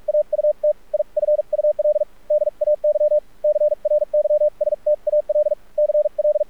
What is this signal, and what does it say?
Morse code, heard as a keyed tone, about 46 wpm, spaced as for 38 wpm, 600 Hz. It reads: AUTIFUL DAY CRYSTAL CL
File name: archived / CRSnet